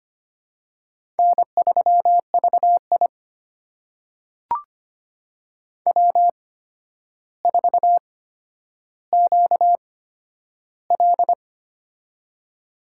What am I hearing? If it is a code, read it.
Morse code, heard as a keyed tone, about 25 wpm, spaced as for 11 wpm, 700 Hz. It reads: N3VI W4QL
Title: USA Call Signs - 4 Characters - 8x Character Spacing - 25wpm